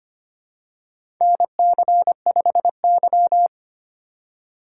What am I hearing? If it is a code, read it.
Morse code, heard as a keyed tone, about 25 wpm, 700 Hz. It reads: NC5Y
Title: USA Call Signs - 2x Character Spacing - 25wpm